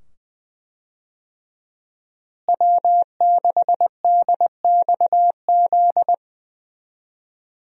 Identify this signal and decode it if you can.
Morse code, heard as a keyed tone, about 20 wpm, 700 Hz. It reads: W6DXZ
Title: USA Call Signs - 5 Characters - Repeated 2x - 20wpm